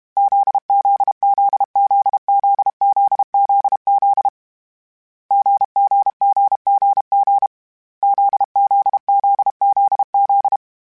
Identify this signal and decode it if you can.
Morse code, heard as a keyed tone, about 32 wpm, 800 Hz. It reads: ZZZZZZZZ GGGGG ZZZZZ
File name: Short Runs / Short Runs-mp3s / GZ GZ7 @30